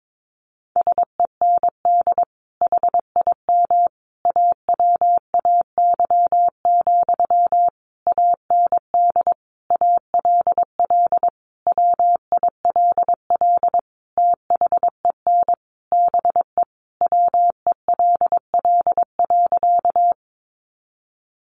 Morse code, heard as a keyed tone, about 22 wpm, 700 Hz. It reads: SEND HIM AWAY, AND ALL WILL THEN BE WELL.
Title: Sentences from Top 1500 Words 22wpm